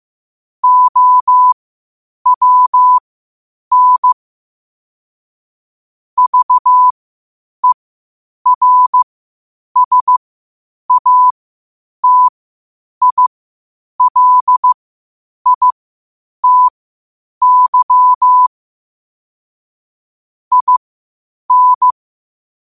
Morse code, heard as a keyed tone, about 15 wpm, spaced as for 8 wpm, 1000 Hz. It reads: OWN VERSATILITY IN
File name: qotdfiles / 8wpm